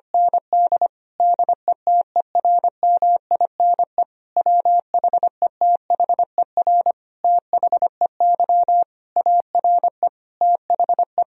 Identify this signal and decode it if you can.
Morse code, heard as a keyed tone, about 25 wpm, 700 Hz. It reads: ND DETERMINE WHETHER THEY ARE THE
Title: Sentences from Top 1000 Words - Review Part 4 - 25wpm